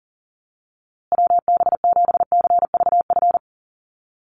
Morse code, heard as a keyed tone, about 37 wpm, 700 Hz. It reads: WB7CVF
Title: USA Call Signs Repeated 2x 40wpm